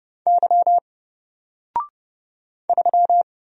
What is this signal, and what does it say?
Morse code, heard as a keyed tone, about 30 wpm, 700 Hz. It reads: Y 3
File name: Single-Letter-Number-30wpm